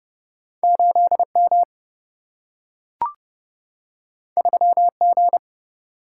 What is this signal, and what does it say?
Morse code, heard as a keyed tone, about 31 wpm, 700 Hz. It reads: 8 M 3 G
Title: Single-Letter Single-Number 30wpm